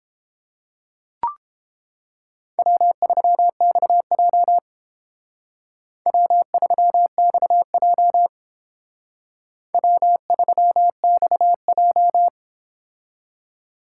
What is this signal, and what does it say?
Morse code, heard as a keyed tone, about 30 wpm, 700 Hz. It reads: W3XJ W3XJ W3XJ